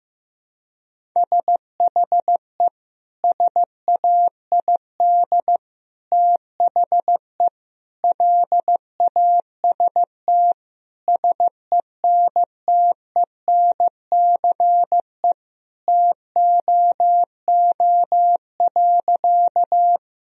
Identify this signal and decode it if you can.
Morse code, heard as a keyed tone, about 16 wpm, 700 Hz. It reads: SHE SAID THE LAST SENTENCE TOO.
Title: Sentences from Top 1000 Words - Review Part 1 - 15wpm